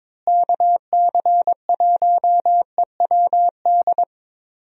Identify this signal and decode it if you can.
Morse code, heard as a keyed tone, about 22 wpm, 700 Hz. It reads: KC1EWD
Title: USA Call Signs - 6 Characters - Repeated 2x - 22wpm